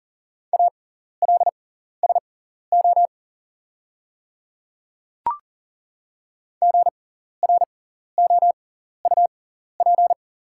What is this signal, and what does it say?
Morse code, heard as a keyed tone, about 40 wpm, spaced as for 14 wpm, 690 Hz. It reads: ALSO GROUP
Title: 6x Character Spacing - Top 500 Words - 40wpm